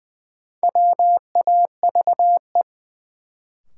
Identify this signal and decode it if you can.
Morse code, heard as a keyed tone, about 20 wpm, 700 Hz. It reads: WAVE
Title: Sentences from Top 1400 Words - New Word Drill Alt - 20wpm